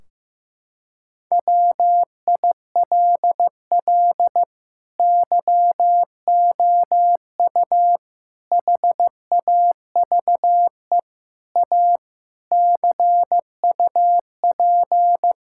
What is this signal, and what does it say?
Morse code, heard as a keyed tone, about 16 wpm, 700 Hz. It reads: WILL YOU HAVE A CUP
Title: Sentences from Top 1600 Words 15wpm